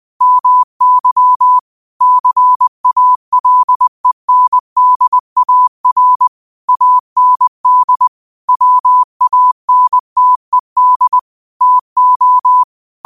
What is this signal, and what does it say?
Morse code, heard as a keyed tone, about 20 wpm, 1000 Hz. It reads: MY CALENDAR AND WANTED TO
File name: qotdfiles / 20wpm